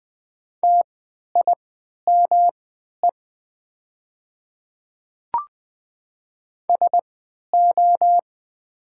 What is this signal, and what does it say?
Morse code, heard as a keyed tone, about 20 wpm, spaced as for 11 wpm, 700 Hz. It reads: TIME SO
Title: 3x Character Spacing - Top 100 Words - 20wpm